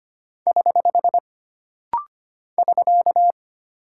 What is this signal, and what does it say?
Morse code, heard as a keyed tone, about 25 wpm, 700 Hz. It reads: <HH> <SK>
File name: Single Punctuation-Prosign - Rapid-Fire - 25wpm